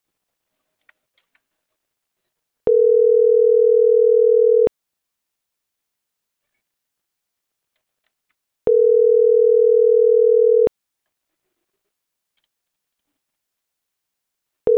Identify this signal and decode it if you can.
Ringback tone (North America), 440 + 480 Hz.